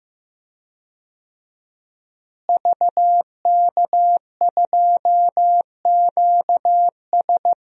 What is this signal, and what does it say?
Morse code, heard as a keyed tone, about 15 wpm, 700 Hz. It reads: VK2QS
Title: DXCC-100 Call Signs - 5 Characters - Repeated 2x - 15wpm